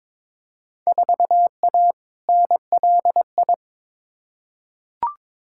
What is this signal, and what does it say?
Morse code, heard as a keyed tone, about 22 wpm, 700 Hz. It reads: 4A NLI
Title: Field Day Exchanges Encore - 22wpm